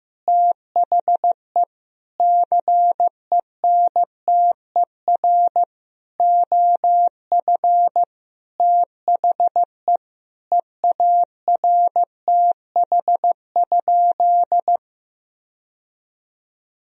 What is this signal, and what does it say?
Morse code, heard as a keyed tone, about 15 wpm, 700 Hz. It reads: THE CENTER OF THE EARTH?
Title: Sentences from Top 1000 Words - Review Part 2 - 15wpm